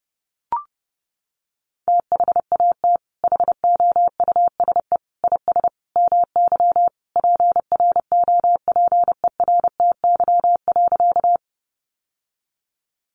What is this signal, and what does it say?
Morse code, heard as a keyed tone, about 30 wpm, 700 Hz. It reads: THAT HOUSE IS MY PROPERTY.
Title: Sentences from Top 1000 Words 30wpm